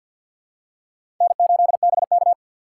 Morse code, heard as a keyed tone, about 47 wpm, 700 Hz. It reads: N8BK
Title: USA Call Signs - 4 Characters - 2x Character Spacing - 50wpm